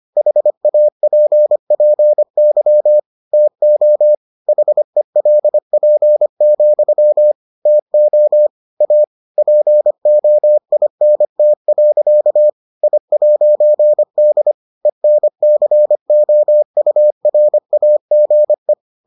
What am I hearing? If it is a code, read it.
Morse code, heard as a keyed tone, about 25 wpm, 600 Hz. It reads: HAPPY TO HELP, TO A POINT. I'D ENCOURAGE